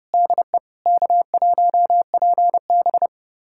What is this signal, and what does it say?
Morse code, heard as a keyed tone, about 30 wpm, 700 Hz. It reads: DE K1PB
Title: USA Callsign DE Callsign - 4 Characters - 30wpm